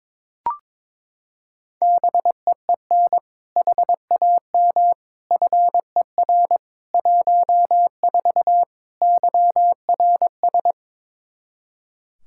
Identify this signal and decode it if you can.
Morse code, heard as a keyed tone, about 22 wpm, 700 Hz. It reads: BEEN HAM FER 14 YRS
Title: Practice QSO - Exchange 3 - More Info - 22wpm